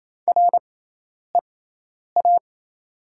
Morse code, heard as a keyed tone, about 28 wpm, spaced as for 10 wpm, 690 Hz. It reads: REA
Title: 6x Character Spacing - Top 500 Words - 28wpm